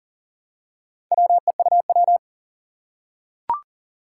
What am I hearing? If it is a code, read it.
Morse code, heard as a keyed tone, about 40 wpm, 700 Hz. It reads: WEUW
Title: Four Challenging Letter-Number Combinations 40wpm